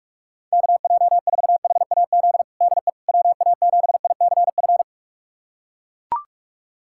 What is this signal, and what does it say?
Morse code, heard as a keyed tone, about 41 wpm, 700 Hz. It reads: KJ4HAZ DE WA7ICF
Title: USA Callsign DE Callsign - 6 Characters - 45wpm